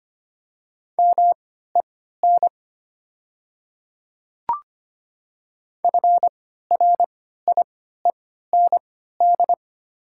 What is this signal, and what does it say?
Morse code, heard as a keyed tone, about 25 wpm, spaced as for 14 wpm, 700 Hz. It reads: MEN FRIEND
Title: 3x Character Spacing - Top 500 Words - 25wpm